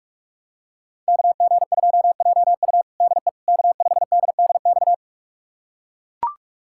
Morse code, heard as a keyed tone, about 41 wpm, 700 Hz. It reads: KG2JU DE K5DDX
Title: USA Callsign DE Callsign - 5 Characters - 45wpm